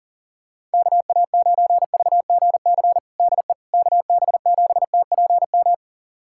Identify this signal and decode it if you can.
Morse code, heard as a keyed tone, about 40 wpm, 700 Hz. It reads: KA9VGC DE KB7TPM
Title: USA Callsign DE Callsign - 6 Characters - 40wpm